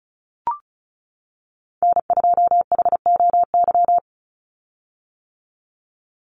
Morse code, heard as a keyed tone, about 34 wpm, 700 Hz. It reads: N2HOY
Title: USA Call Signs 35wpm